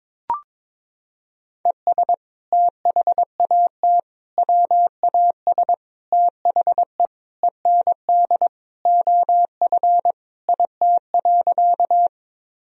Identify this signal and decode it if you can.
Morse code, heard as a keyed tone, about 22 wpm, 700 Hz. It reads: ES THAT WAS THE END OF IT.
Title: Sentences from Top 200 Words and Abbreviations 22wpm